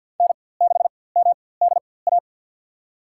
Morse code, heard as a keyed tone, about 47 wpm, spaced as for 23 wpm, 700 Hz. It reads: N6MDA
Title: USA Call Signs - 5 Characters - 4x Character Spacing - 50wpm